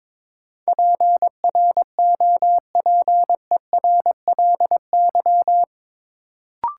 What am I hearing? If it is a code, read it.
Morse code, heard as a keyed tone, about 22 wpm, 700 Hz. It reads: PROPERLY
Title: Sentences from Top 1100 Words - New Word Drill Alt - 22wpm